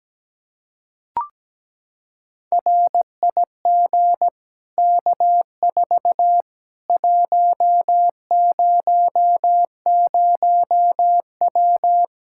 Morse code, heard as a keyed tone, about 17 wpm, 700 Hz. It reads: RIG K4 100W
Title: Practice QSO - Exchange 2 - Info - 17wpm